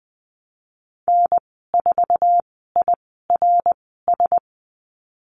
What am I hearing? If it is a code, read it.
Morse code, heard as a keyed tone, about 20 wpm, spaced as for 14 wpm, 700 Hz. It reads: N4IRS
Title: 082 Lesson 14g - Number 2 - Call Signs 20wpm